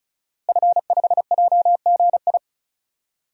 Morse code, heard as a keyed tone, about 36 wpm, 700 Hz. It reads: F5JGI
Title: DXCC-100 Call Signs - 5 Characters - Repeated 2x - 35wpm